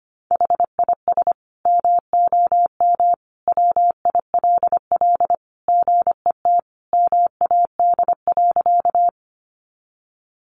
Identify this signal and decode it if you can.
Morse code, heard as a keyed tone, about 25 wpm, 700 Hz. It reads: HIS MOM WILL GET MAD.
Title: Additional Easy Phrases 25wpm